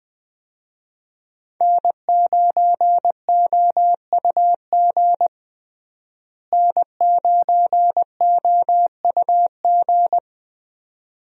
Morse code, heard as a keyed tone, about 21 wpm, 700 Hz. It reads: N9OUG N9OUG
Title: USA Call Signs - 5 Characters - Repeated 2x - 20wpm